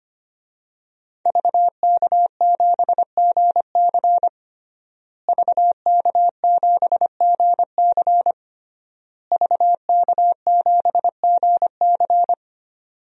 Morse code, heard as a keyed tone, about 25 wpm, 700 Hz. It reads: VK7GC VK7GC VK7GC
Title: DXCC-100 Call Signs - 5 Characters - Repeated 3x - 25wpm